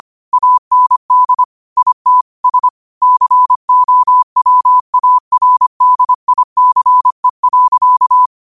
Morse code, heard as a keyed tone, about 25 wpm, 1000 Hz. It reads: AND IT S COWARDICE.